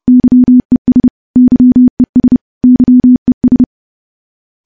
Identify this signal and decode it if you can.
Morse code, heard as a keyed tone, about 30 wpm, 250 Hz. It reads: YES YES YES
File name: morse / morse-training-30WPM